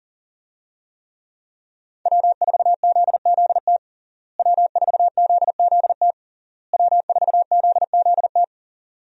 Morse code, heard as a keyed tone, about 37 wpm, 700 Hz. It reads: W4ZZT W4ZZT W4ZZT
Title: USA Call Signs - 5 Characters - Repeated 3x - 40wpm